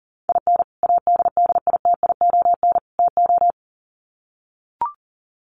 Morse code, heard as a keyed tone, about 40 wpm, 700 Hz. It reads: IN ADDITION TO
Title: Sets of 3 Words 40wpm